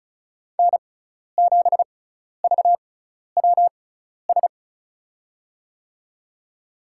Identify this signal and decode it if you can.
Morse code, heard as a keyed tone, about 36 wpm, spaced as for 12 wpm, 700 Hz. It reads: N7VWS
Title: USA Call Signs - 5 Characters - 6x Character Spacing - 35wpm